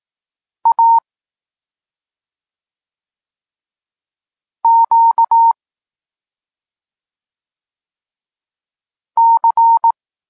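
Morse code, heard as a keyed tone, about 18 wpm, 920 Hz. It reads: A Q C
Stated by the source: Polish Morse lesson teaching the letter C, from SP9MOA Niepołomicki Klub Krótkofalowców.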